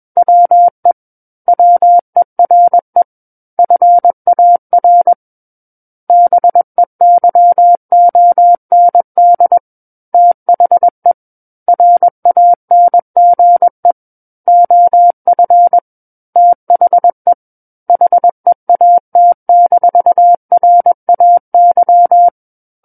Morse code, heard as a keyed tone, about 21 wpm, 700 Hz. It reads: WE WERE FAR BEYOND THE RANGE OF THE HEAT-RAY
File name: Book One / 12-Chapter 12 - 21 WPM